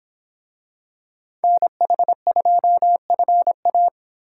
Morse code, heard as a keyed tone, about 26 wpm, 700 Hz. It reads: NH2FA